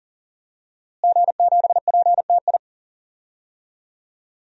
Morse code, heard as a keyed tone, about 40 wpm, 700 Hz. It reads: G7PTI
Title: DXCC-100 Call Signs - 5 Characters - Repeated 3x - 40wpm